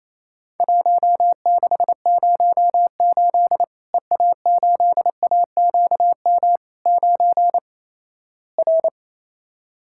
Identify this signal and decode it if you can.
Morse code, heard as a keyed tone, about 28 wpm, 700 Hz. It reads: 1608 EA8AQM 9 R
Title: Worked All Europe DX Contest - QTC Exchange - 28wpm